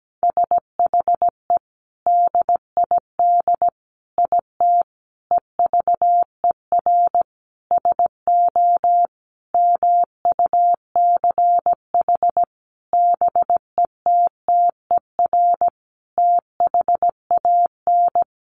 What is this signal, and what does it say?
Morse code, heard as a keyed tone, about 17 wpm, 700 Hz. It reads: SHE DID IT EVER SO MUCH BETTER THAN
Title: Sentences from Top 400 Words 17wpm